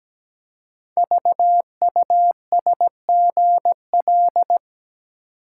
Morse code, heard as a keyed tone, about 17 wpm, 700 Hz. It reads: VUSGL
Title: Code Groups - 5 Letters - 17wpm